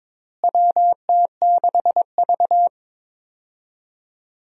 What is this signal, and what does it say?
Morse code, heard as a keyed tone, about 22 wpm, 700 Hz. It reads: WT6V
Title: USA Call Signs - 4 Characters - Repeated 3x - 22wpm